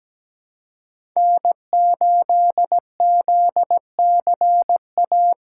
Morse code, heard as a keyed tone, about 17 wpm, 700 Hz. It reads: N8ZCA
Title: USA Call Signs - 5 Characters - 3x Character Spacing - 17wpm